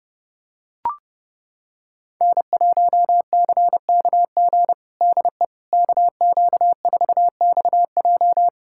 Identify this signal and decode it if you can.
Morse code, heard as a keyed tone, about 30 wpm, 700 Hz. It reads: N1CKG DE KQ4XJ
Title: USA Callsign DE Callsign - 5 Characters - 30wpm